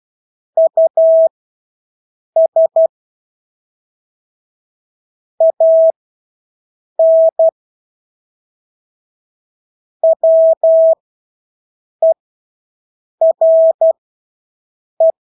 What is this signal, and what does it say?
Morse code, heard as a keyed tone, about 12 wpm, spaced as for 6 wpm, 650 Hz. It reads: US AN WERE
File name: Engelske ord 12-6 wpm